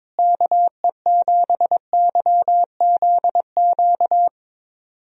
Morse code, heard as a keyed tone, about 22 wpm, 700 Hz. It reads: KE7YZQ
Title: USA Call Signs - 6 Characters - Repeated 2x - 22wpm